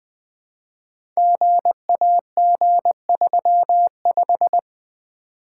Morse code, heard as a keyed tone, about 21 wpm, 700 Hz. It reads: GAG35
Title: Code Groups - 5 Letter:Numbers - 20wpm